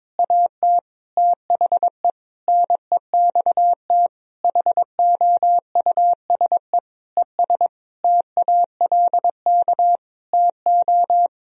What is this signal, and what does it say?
Morse code, heard as a keyed tone, about 22 wpm, 700 Hz. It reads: AT THE NEXT HOUSE ES TALK TO
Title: Sentences from Top 400 Words and Abbreviations 22wpm